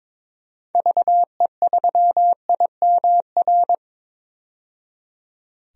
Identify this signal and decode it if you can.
Morse code, heard as a keyed tone, about 22 wpm, 700 Hz. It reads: VE3IMR
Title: DXCC-100 Call Signs - 6 Characters - 22wpm